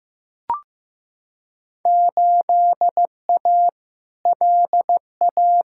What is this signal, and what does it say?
Morse code, heard as a keyed tone, about 15 wpm, 700 Hz. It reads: 8A LA
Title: Field Day Exchanges Encore - 15wpm